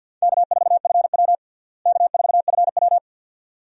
Morse code, heard as a keyed tone, about 51 wpm, 700 Hz. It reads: K4FW K4FW
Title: USA Call Signs - 4 Characters - Repeated 2x - 50wpm